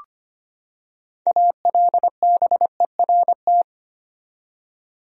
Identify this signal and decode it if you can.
Morse code, heard as a keyed tone, about 25 wpm, 700 Hz. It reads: ALBERT
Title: USA Common Names - 25wpm